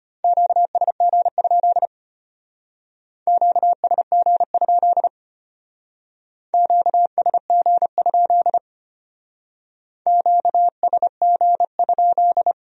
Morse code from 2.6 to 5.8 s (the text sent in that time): QSG?